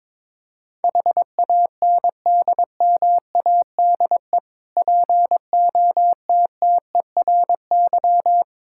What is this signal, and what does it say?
Morse code, heard as a keyed tone, about 22 wpm, 690 Hz. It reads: HANDMADE POTTERY
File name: Difficult Words with Hints - 22wpm